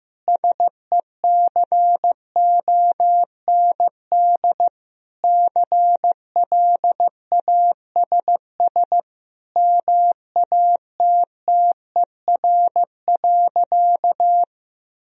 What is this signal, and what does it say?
Morse code, heard as a keyed tone, about 15 wpm, 700 Hz. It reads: SECOND CLASS MATTER.
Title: Sentences from Top 1000 Words - Review Part 2 - 15wpm